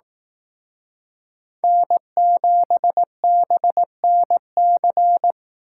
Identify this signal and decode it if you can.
Morse code, heard as a keyed tone, about 19 wpm, 700 Hz. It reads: N7BNC